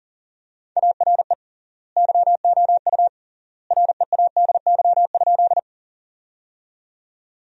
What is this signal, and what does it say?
Morse code, heard as a keyed tone, about 40 wpm, 700 Hz. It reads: ARE YOU READY?
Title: 3x Word-Spacing - Sentences from Top 400 Words - 40wpm